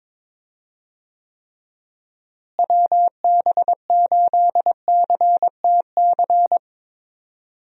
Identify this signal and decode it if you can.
Morse code, heard as a keyed tone, about 21 wpm, 700 Hz. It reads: WB8CTC
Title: USA Call Signs - 6 Characters - Repeated 3x - 22wpm